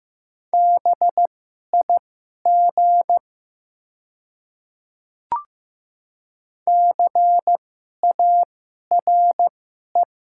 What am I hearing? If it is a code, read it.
Morse code, heard as a keyed tone, about 15 wpm, spaced as for 11 wpm, 700 Hz. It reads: BIG CARE
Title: 2x Character Spacing - Top 500 Words - 15wpm